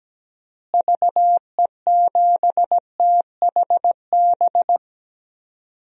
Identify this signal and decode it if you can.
Morse code, heard as a keyed tone, about 17 wpm, 700 Hz. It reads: VE7THB
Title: DXCC-100 Call Signs - 6 Characters - Repeated 2x - 17wpm